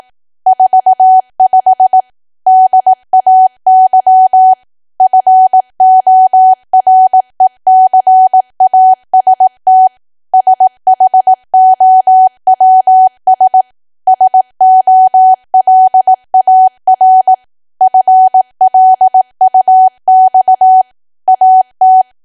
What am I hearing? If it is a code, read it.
Morse code, heard as a keyed tone, about 18 wpm, 750 Hz. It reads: 45 DAY FORECAST SHOWS SOLAR FLUX AT
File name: cwa / W1AW-Level2 / 18WPM